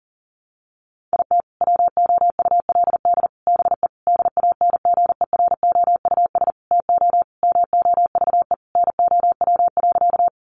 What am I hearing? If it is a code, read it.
Morse code, heard as a keyed tone, about 40 wpm, 700 Hz. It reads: IT WOULD BE DANGEROUS TO MOVE NOW.
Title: Sentences from Top 1200 Words 40wpm